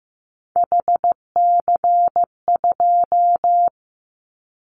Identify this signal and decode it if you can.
Morse code, heard as a keyed tone, about 15 wpm, 700 Hz. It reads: HC2
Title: Callsign Prefixes for Top 100 DXCC - 15wpm